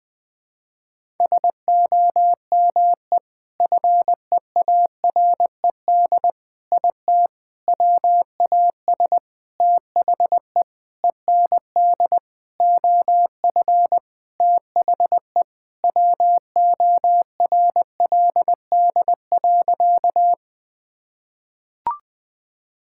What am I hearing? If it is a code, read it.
Morse code, heard as a keyed tone, about 21 wpm, 700 Hz. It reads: SOME FEARED IT WAS THE END OF THE WORLD.
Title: Sentences with 5-Word Phrases - 20wpm